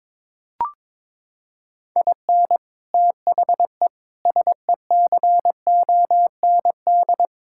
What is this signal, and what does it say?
Morse code, heard as a keyed tone, about 22 wpm, 700 Hz. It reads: IN THE SECOND
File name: practice / 22wpm / Sets of 6 Words - 22wpm